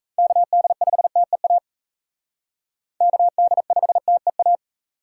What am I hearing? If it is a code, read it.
Morse code, heard as a keyed tone, about 41 wpm, 700 Hz. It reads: KD5TEA KD5TEA